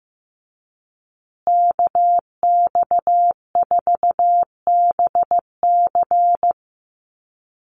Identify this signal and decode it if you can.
Morse code, heard as a keyed tone, about 15 wpm, 700 Hz. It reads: KX4BC
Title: USA Call Signs Repeated 2x 15wpm